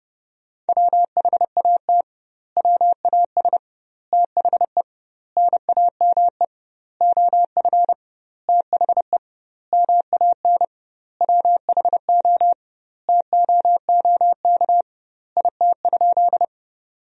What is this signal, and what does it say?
Morse code, heard as a keyed tone, about 30 wpm, 700 Hz. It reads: WHAT WAS THE NAME OF THE MAN WHO TOOK IT?
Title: 2x Word-Spacing - Sentences from Top 300 Words - 30wpm